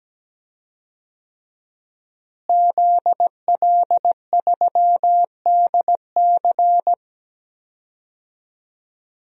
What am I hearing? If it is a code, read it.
Morse code, heard as a keyed tone, about 17 wpm, 700 Hz. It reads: ZL3DC
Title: DXCC-100 Call Signs - 5 Characters - Repeated 3x - 17wpm